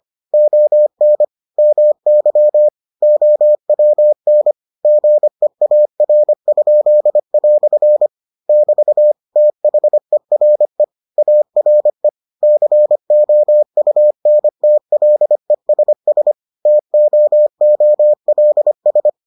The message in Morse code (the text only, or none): ON MY OWN GEAR?" <BT> THERE ARE COUNTLESS TOOLS